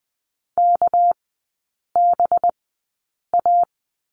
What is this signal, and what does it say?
Morse code, heard as a keyed tone, about 20 wpm, spaced as for 8 wpm, 700 Hz. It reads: KBA
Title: 198 Lesson 32e - Letter k - Three Characters 20wpm